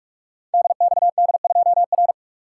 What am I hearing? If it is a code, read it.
Morse code, heard as a keyed tone, about 46 wpm, 700 Hz. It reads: DXD2R
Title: Code Groups - 5 Letter:Numbers - 45wpm